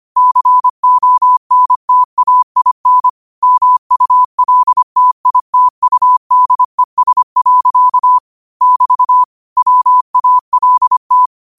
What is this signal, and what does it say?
Morse code, heard as a keyed tone, about 25 wpm, 1000 Hz. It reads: CONTAIN MULTITUDES. <BT> WALT